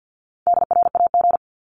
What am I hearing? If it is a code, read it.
Morse code, heard as a keyed tone, about 50 wpm, 700 Hz. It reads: DRAG
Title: Top 1251-1500 Words 50wpm